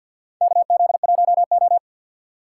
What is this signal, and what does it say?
Morse code, heard as a keyed tone, about 50 wpm, 700 Hz. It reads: KZ1O
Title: USA Call Signs - 4 Characters - 2x Character Spacing - 50wpm